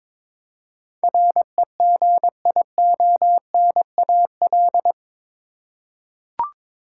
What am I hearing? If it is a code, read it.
Morse code, heard as a keyed tone, about 21 wpm, 700 Hz. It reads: REGIONAL
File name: practice / 22wpm / Eight Letter Words - 22wpm